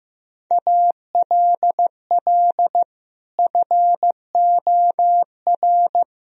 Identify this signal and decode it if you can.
Morse code, heard as a keyed tone, about 16 wpm, 700 Hz. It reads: ALL FOR
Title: Sentences from Top 1000 Words - Review Part 2 - 15wpm